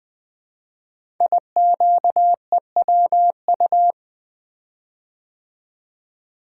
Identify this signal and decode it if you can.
Morse code, heard as a keyed tone, about 20 wpm, 700 Hz. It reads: IQEWU
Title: Code Groups - 5 Letters - 20wpm